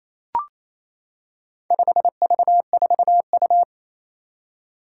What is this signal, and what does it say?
Morse code, heard as a keyed tone, about 29 wpm, 700 Hz. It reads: 5V4U
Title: Four Challenging Letter-Number Combinations 28wpm